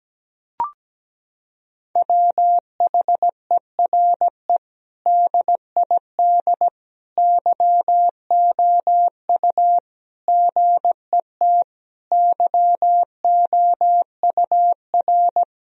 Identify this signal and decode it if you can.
Morse code, heard as a keyed tone, about 17 wpm, 700 Hz. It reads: WHERE DID YOU GET YOUR
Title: Sentences from Top 1000 Words - Review Part 4 - 17wpm